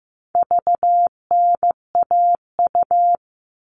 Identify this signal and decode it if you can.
Morse code, heard as a keyed tone, about 15 wpm, 700 Hz. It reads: VNAU
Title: All Cut Numbers - 4-Digit 15wpm